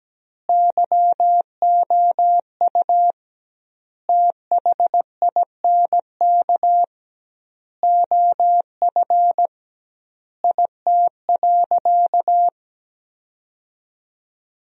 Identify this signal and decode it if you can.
Morse code, heard as a keyed tone, about 17 wpm, 700 Hz. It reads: YOU THINK OF IT.
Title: 2x Word-Spacing - Sentences from Top 200 Words - 17wpm